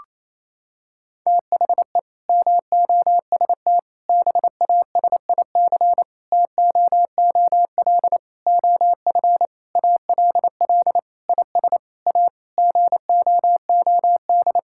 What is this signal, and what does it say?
Morse code, heard as a keyed tone, about 28 wpm, 700 Hz. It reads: THE MOST BASIC TOOL OF ALL IS A GOOD
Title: Sentences from Top 1000 Words - Review Part 6 - 28wpm